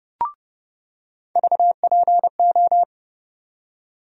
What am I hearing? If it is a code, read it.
Morse code, heard as a keyed tone, about 32 wpm, 700 Hz. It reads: VPO
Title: USA Call Sign Postfixes - 3 Letters 30wpm